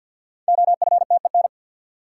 Morse code, heard as a keyed tone, about 47 wpm, 700 Hz. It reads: OFTEN
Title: Sentences from Top 300 Words - New Word Drill - 50wpm